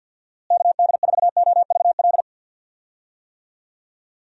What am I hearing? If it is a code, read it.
Morse code, heard as a keyed tone, about 50 wpm, 700 Hz. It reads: KD4OVL
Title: USA Call Signs - 6 Characters - Repeated 3x - 50wpm